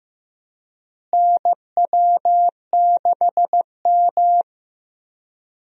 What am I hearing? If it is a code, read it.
Morse code, heard as a keyed tone, about 15 wpm, 700 Hz. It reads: NW6M
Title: USA Call Signs - 4 Characters - Repeated 3x - 15wpm